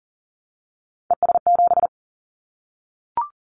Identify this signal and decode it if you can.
Morse code, heard as a keyed tone, about 40 wpm, 700 Hz. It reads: ES7
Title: Callsign Prefixes for Top 50 DXCC - 40wpm